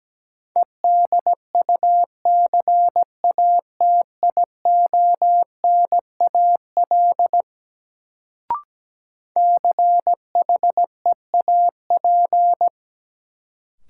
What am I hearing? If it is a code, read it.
Morse code, heard as a keyed tone, about 17 wpm, 700 Hz. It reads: EDUCATIONAL CHEAP
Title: Sentences from Top 1200 Words - New Word Drill Alt - 17wpm